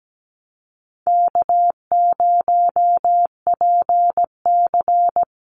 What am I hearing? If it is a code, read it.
Morse code, heard as a keyed tone, about 17 wpm, 700 Hz. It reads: K0PC
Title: Easy Call Signs Repeated 3x - 17wpm